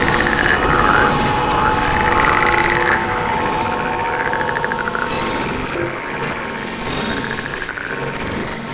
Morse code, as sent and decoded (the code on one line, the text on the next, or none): none